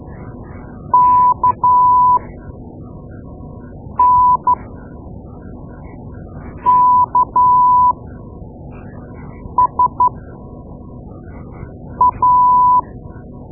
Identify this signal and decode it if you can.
Morse code, heard as a keyed tone, about 12 wpm, spaced as for 4 wpm, 990 Hz. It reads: KNKSA
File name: Morse Lessons / Morse Lessons / Tape-2A